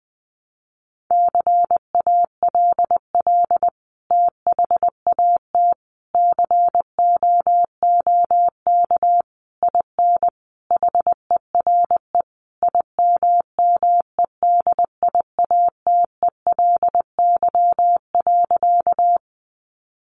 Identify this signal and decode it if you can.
Morse code, heard as a keyed tone, about 20 wpm, 700 Hz. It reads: CALL THAT COOK IN HERE IMMEDIATELY.
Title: Sentences from Top 1100 Words 20wpm